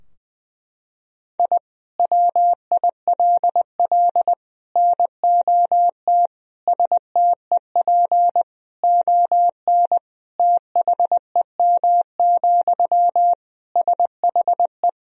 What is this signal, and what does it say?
Morse code, heard as a keyed tone, about 20 wpm, 700 Hz. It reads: I WILL NOT STEP ON THEM, SHE
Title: Sentences from Top 1000 Words - Review Part 2 - 20wpm